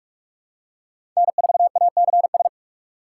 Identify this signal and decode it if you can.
Morse code, heard as a keyed tone, about 43 wpm, 690 Hz. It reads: N4ACS
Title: USA Call Signs - 5 Characters - 2x Character Spacing - 45wpm